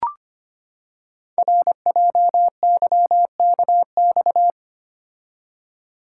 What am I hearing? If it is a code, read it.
Morse code, heard as a keyed tone, about 25 wpm, 700 Hz. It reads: RJYKX
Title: Code Groups - 5 Letters - 25wpm